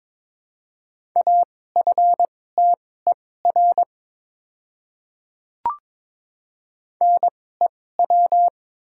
Morse code, heard as a keyed tone, about 22 wpm, spaced as for 16 wpm, 700 Hz. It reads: AFTER NEW
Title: 2x Character Spacing - Top 300 Words - 22wpm